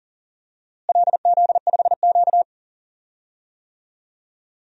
Morse code, heard as a keyed tone, about 40 wpm, 700 Hz. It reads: LZ5Q
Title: DXCC-100 Call Signs - 4 Characters - Repeated 2x - 40wpm